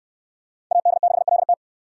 Morse code, heard as a keyed tone, about 69 wpm, 690 Hz. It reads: AD6LT